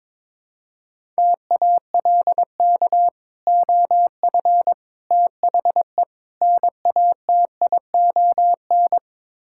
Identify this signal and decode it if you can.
Morse code, heard as a keyed tone, about 22 wpm, 700 Hz. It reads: TALK OF THE NATION
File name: practice / 22wpm / Build A Phrase - 5-Word Phrases - 22wpm